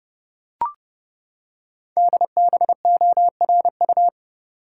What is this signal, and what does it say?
Morse code, heard as a keyed tone, about 30 wpm, 700 Hz. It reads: DBORU
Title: Code Groups - 5 Letters - 30wpm